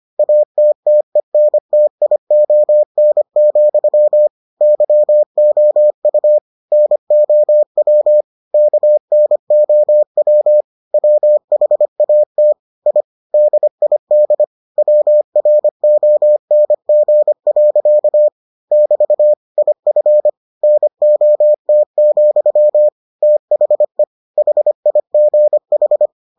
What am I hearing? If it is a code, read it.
Morse code, heard as a keyed tone, about 25 wpm, 600 Hz. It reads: ATTENTION, YOU NOW KNOW WHAT I DID WRONG. <BT> IF NOT, THE HIGH